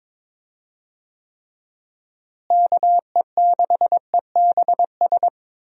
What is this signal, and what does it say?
Morse code, heard as a keyed tone, about 22 wpm, 700 Hz. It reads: KE6EBS
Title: USA Call Signs - 6 Characters - Repeated 3x - 22wpm